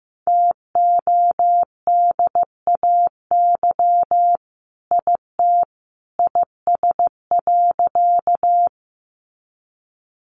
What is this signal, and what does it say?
Morse code, heard as a keyed tone, about 15 wpm, 700 Hz. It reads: TODAY IT IS.
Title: Sentences from Top 600 Words 15wpm